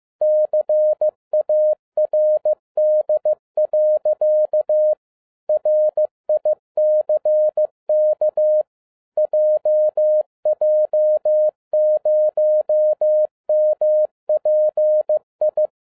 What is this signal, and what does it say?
Morse code, heard as a keyed tone, about 15 wpm, 610 Hz. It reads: CARD. RICK JJ0MPI
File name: 2012-01-mp3 / n13-A NEWLY DESIGNED QSL CARD-15wpm